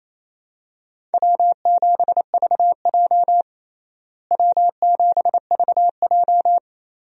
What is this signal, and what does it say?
Morse code, heard as a keyed tone, about 28 wpm, 700 Hz. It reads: W7VJ W7VJ
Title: USA Call Signs - 4 Characters - Repeated 2x - 28wpm